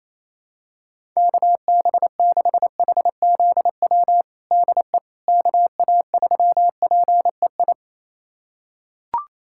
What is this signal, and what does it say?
Morse code, heard as a keyed tone, about 28 wpm, 700 Hz. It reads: KB6HZW DE KA3PEI
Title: USA Callsign DE Callsign - 6 Characters - 28wpm